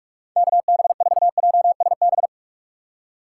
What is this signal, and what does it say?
Morse code, heard as a keyed tone, about 47 wpm, 700 Hz. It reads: KB4JSB
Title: USA Call Signs - 4x Character Spacing - 45wpm